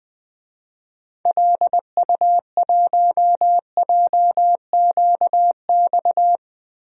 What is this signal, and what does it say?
Morse code, heard as a keyed tone, about 20 wpm, 700 Hz. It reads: LU1JQX
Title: DXCC-100 Call Signs - 6 Characters - 20wpm